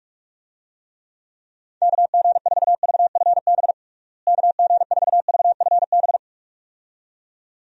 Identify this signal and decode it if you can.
Morse code, heard as a keyed tone, about 46 wpm, 700 Hz. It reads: KG4VFB KG4VFB
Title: USA Call Signs - 6 Characters - Repeated 2x - 45wpm